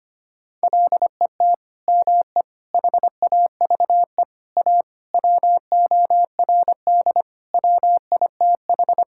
Morse code, heard as a keyed tone, about 25 wpm, 700 Hz. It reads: LET ME HAVE A WORD WITH
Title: Sentences from Top 1000 Words - Review Part 6 - 25wpm